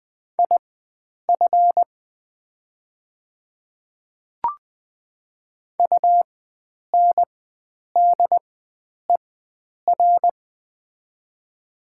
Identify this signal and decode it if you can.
Morse code, heard as a keyed tone, about 20 wpm, spaced as for 9 wpm, 700 Hz. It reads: IF UNDER
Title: 4x Character Spacing - Top 500 Words - 20wpm